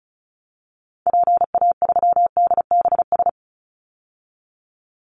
Morse code, heard as a keyed tone, about 36 wpm, 700 Hz. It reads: PA3DBS
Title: Call Signs - 35wpm